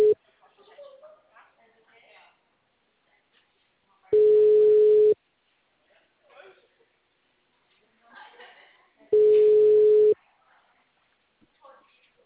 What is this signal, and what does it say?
Ringback tone (Europe, 425 Hz), with speech.
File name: out- -2053